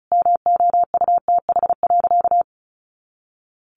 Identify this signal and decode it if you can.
Morse code, heard as a keyed tone, about 34 wpm, 700 Hz. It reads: MOUTH.
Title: Easy Phrases 35wpm